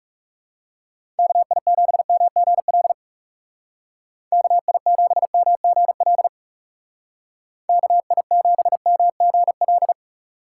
Morse code, heard as a keyed tone, about 36 wpm, 700 Hz. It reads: KI7MGL KI7MGL KI7MGL
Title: Speed Racing - USA 2x3 Call Signs 30wpm